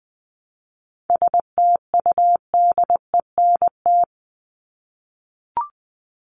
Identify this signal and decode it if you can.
Morse code, heard as a keyed tone, about 20 wpm, 700 Hz. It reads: STUDENT
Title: ES:S Suffix - Top 100 Words with Base Words 20wpm